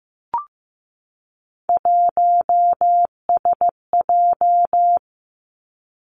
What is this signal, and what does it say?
Morse code, heard as a keyed tone, about 15 wpm, 700 Hz. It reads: 1SJ
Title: Three Challenging Letter-Number Combinations 15wpm